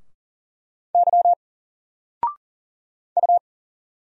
Morse code, heard as a keyed tone, about 40 wpm, 700 Hz. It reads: Y U
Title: Single Letter-Number 40wpm